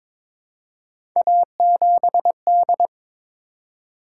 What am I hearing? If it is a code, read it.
Morse code, heard as a keyed tone, about 22 wpm, 700 Hz. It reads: A7D
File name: practice / 22wpm / Three Letter-Number Combinations - 22wpm